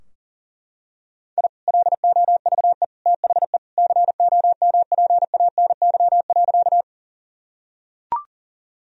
Morse code, heard as a keyed tone, about 40 wpm, 700 Hz. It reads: I LOVE THE COMPANY.
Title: Sentences from Top 1000 Words - Review Part 3 - 40wpm